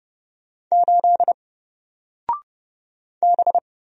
Morse code, heard as a keyed tone, about 30 wpm, 700 Hz. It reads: 8 B
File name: Single-Letter-Number-30wpm